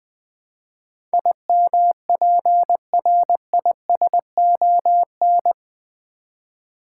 Morse code, heard as a keyed tone, about 20 wpm, 700 Hz. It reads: IMPRISON
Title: Build A Word - 3 Letters At A Time - 8-Letter Words - 20wpm